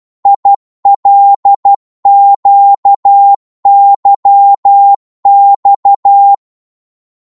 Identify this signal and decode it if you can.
Morse code, heard as a keyed tone, about 12 wpm, 800 Hz. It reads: ILQYX
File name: Morse / ltr-12wpm